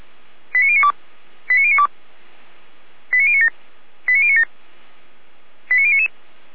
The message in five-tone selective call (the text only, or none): 89092 89092 89097 89097 89099